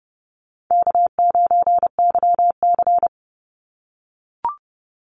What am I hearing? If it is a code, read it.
Morse code, heard as a keyed tone, about 30 wpm, 700 Hz. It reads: K9YC
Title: Easy Call Signs - 30wpm